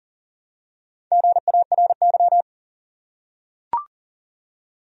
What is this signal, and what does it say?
Morse code, heard as a keyed tone, about 40 wpm, 700 Hz. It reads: GARY
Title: USA Common Names - 40wpm